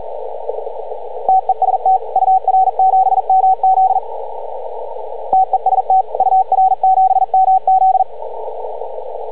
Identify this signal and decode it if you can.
Morse code, heard as a keyed tone, about 42 wpm, 750 Hz, with radio noise. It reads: TEST UR7MZ TEST UR7MZ
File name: CW / ha6oi 2